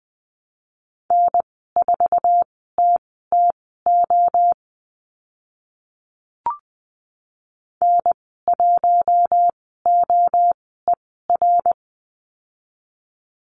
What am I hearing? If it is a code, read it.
Morse code, heard as a keyed tone, about 20 wpm, spaced as for 14 wpm, 700 Hz. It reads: N4TTO N1OER
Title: 062 Lesson 11g - Letter h - Call Signs 20wpm